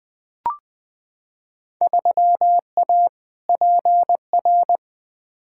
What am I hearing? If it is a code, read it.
Morse code, heard as a keyed tone, about 20 wpm, 700 Hz. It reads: 3A PR
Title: Field Day Exchanges Encore - 20wpm